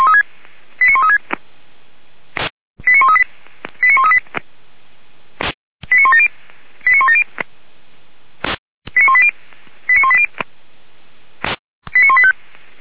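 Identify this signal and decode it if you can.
Five-tone selective call (ZVEI), heard as quick runs of five tones, tones of about 70 ms. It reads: 89136 89138 89138 89170 89170 89180 89180 89185
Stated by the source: ZVEI five-tone selective call received with an SDR for Feuerwehr Orlamünde, a fire brigade.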